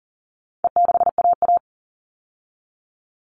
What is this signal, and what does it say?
Morse code, heard as a keyed tone, about 40 wpm, 700 Hz. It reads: E6AA
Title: All Cut Numbers - 4-Digit 40wpm